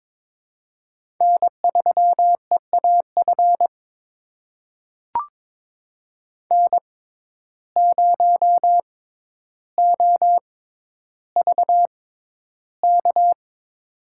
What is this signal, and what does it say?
Morse code, heard as a keyed tone, about 22 wpm, spaced as for 11 wpm, 700 Hz. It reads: N3EAF N0OVK
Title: USA Call Signs - 5 Characters - 6x Character Spacing - 22wpm